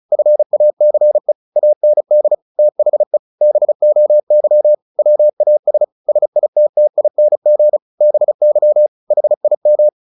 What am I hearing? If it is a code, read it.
Morse code, heard as a keyed tone, about 34 wpm, 600 Hz. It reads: FACE AND THE BOY WAS SITTING BY HIM
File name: the old man and the sea 2 35wpm